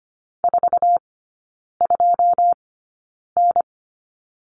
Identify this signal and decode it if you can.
Morse code, heard as a keyed tone, about 25 wpm, spaced as for 9 wpm, 690 Hz. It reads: 42N
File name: Lesson 14e - Number 2 - Three Characters 25wpm